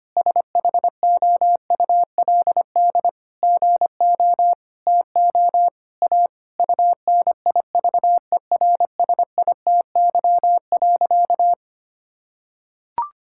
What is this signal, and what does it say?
Morse code, heard as a keyed tone, about 25 wpm, 700 Hz. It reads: SHOULD GO TO A UNIVERSITY.
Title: Sentences from Top 1000 Words - Review Part 5 - 25wpm